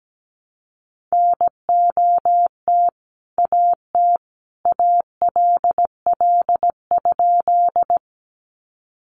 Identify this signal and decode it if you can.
Morse code, heard as a keyed tone, about 17 wpm, 700 Hz. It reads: NOT AT ALL?
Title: Sentences from Top 600 Words 17wpm